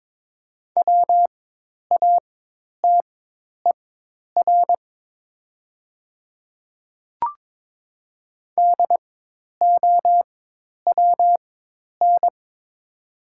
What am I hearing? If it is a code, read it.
Morse code, heard as a keyed tone, about 22 wpm, spaced as for 10 wpm, 700 Hz. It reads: WATER DOWN
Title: 4x Character Spacing - Top 100 Words - 22wpm